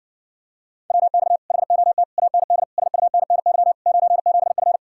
Morse code, heard as a keyed tone, about 50 wpm, 700 Hz. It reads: WX HOT AND SUNNY 97F